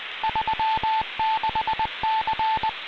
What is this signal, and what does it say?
Morse code, heard as a keyed tone, about 20 wpm, 880 Hz, with noise. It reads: 36C